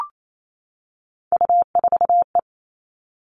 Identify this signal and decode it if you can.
Morse code, heard as a keyed tone, about 28 wpm, 700 Hz. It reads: U4E
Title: All Cut Numbers - 3-Digit 28wpm